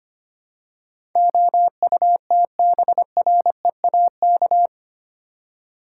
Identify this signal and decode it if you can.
Morse code, heard as a keyed tone, about 25 wpm, 700 Hz. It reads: OUTBREAK
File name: Eight Letter Words - 25wpm